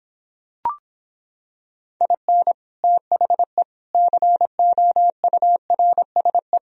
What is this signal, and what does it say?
Morse code, heard as a keyed tone, about 26 wpm, 700 Hz. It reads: IN THE COURSE